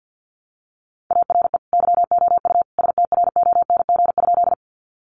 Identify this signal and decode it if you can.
Morse code, heard as a keyed tone, about 50 wpm, 700 Hz. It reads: ARE YOU STRONG?
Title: Sentences from Top 500 Words 50wpm